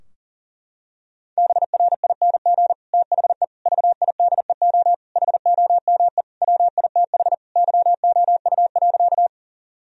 Morse code, heard as a keyed tone, about 40 wpm, 700 Hz. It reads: BRING THE VIDEO HOME WITH YOU.
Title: Sentences from Top 1000 Words - Review Part 4 - 40wpm